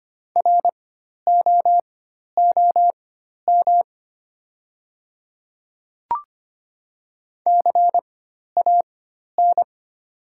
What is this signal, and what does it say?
Morse code, heard as a keyed tone, about 25 wpm, spaced as for 12 wpm, 700 Hz. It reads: ROOM CAN
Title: 4x Character Spacing - Top 300 Words - 25wpm